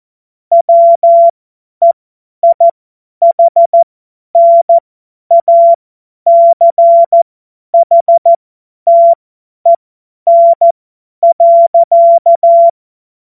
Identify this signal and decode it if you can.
Morse code, heard as a keyed tone, about 14 wpm, spaced as for 10 wpm, 680 Hz. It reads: WEIHNACHTEN.
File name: Morsekurs / Witze / Witz 31. Januar 2023 50 F70